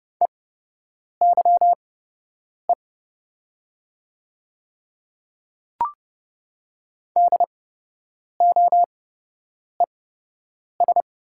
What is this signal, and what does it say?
Morse code, heard as a keyed tone, about 30 wpm, spaced as for 8 wpm, 700 Hz. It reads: EYE DOES
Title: 8x Character Spacing - Top 300 Words - 30wpm